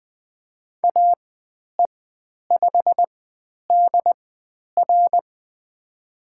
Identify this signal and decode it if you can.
Morse code, heard as a keyed tone, about 20 wpm, spaced as for 10 wpm, 700 Hz. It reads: AE5DR
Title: 102 Lesson 17g - Letter u - Call Signs 20wpm